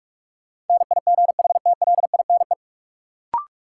Morse code, heard as a keyed tone, about 46 wpm, 700 Hz. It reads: NIGHTLINE
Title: Sets of 5 Words - New Word Drill - 45wpm